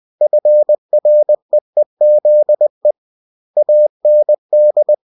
Morse code, heard as a keyed tone, about 20 wpm, 600 Hz. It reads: FREEZE AND
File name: cw-books / winnie-the-pooh / winnie-20wpm-page-0068